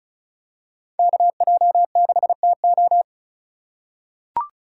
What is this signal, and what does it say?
Morse code, heard as a keyed tone, about 34 wpm, 700 Hz. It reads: KJ6TO
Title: USA Call Signs - 5 Characters - 4x Character Spacing - 35wpm